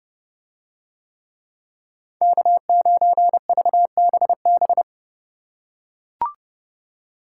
Morse code, heard as a keyed tone, about 29 wpm, 700 Hz. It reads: K9VBB